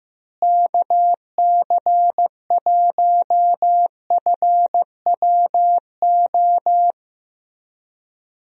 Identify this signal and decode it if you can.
Morse code, heard as a keyed tone, about 15 wpm, 700 Hz. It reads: KC1FWO
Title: USA Call Signs - 6 Characters - Repeated 3x - 15wpm